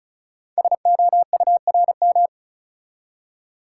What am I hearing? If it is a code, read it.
Morse code, heard as a keyed tone, about 34 wpm, 690 Hz. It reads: SOURM
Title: Code Groups - 5 Letters - 35wpm